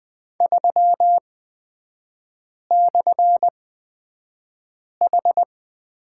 Morse code, heard as a keyed tone, about 20 wpm, spaced as for 5 wpm, 700 Hz. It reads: 3/H
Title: 250 Lesson 40e - BK - Three Characters 20wpm